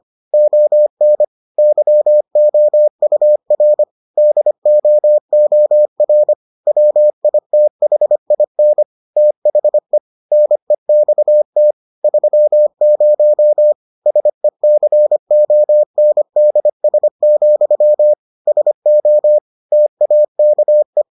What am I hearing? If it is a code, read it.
Morse code, heard as a keyed tone, about 25 wpm, 600 Hz. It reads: ON YOUR DOOR WITHIN THE NEXT 30 SECONDS, SO TAKE